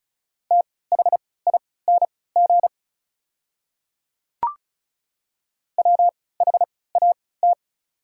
Morse code, heard as a keyed tone, about 36 wpm, spaced as for 20 wpm, 700 Hz. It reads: THING WHAT
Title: 3x Character Spacing - Top 200 Words - 35wpm